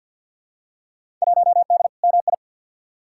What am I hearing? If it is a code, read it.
Morse code, heard as a keyed tone, about 53 wpm, 700 Hz. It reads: 1D MI
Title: Field Day Exchanges - 50wpm